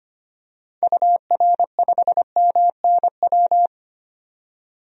Morse code, heard as a keyed tone, about 25 wpm, 700 Hz. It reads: UR5MNW
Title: DXCC-100 Call Signs - 6 Characters - 25wpm